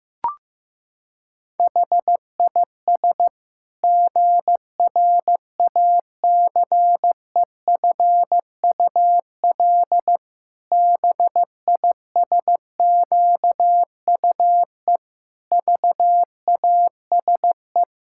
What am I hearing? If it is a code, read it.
Morse code, heard as a keyed tone, about 15 wpm, 700 Hz. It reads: HIS GRACEFUL BISQUE VASE
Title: Pangram Challenge v2 - 15wpm